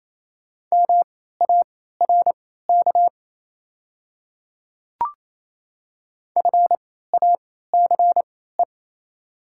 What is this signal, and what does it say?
Morse code, heard as a keyed tone, about 28 wpm, spaced as for 16 wpm, 700 Hz. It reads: MARK FACE
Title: 3x Character Spacing - Top 500 Words - 28wpm